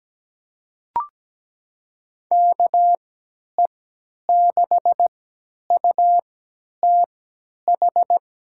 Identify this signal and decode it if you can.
Morse code, heard as a keyed tone, about 17 wpm, spaced as for 10 wpm, 700 Hz. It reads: KE6UTH
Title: USA Call Signs - 6 Characters - 3x Character Spacing - 17wpm